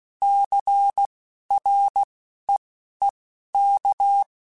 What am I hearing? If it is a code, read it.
Morse code, heard as a keyed tone, about 16 wpm, spaced as for 11 wpm, 800 Hz. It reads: CREEK